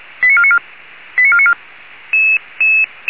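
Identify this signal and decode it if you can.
Five-tone selective call (ZVEI), heard as quick runs of five tones, tones of about 70 ms. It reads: 89494 89494
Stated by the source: ZVEI five-tone selective call 89496, received by SDR.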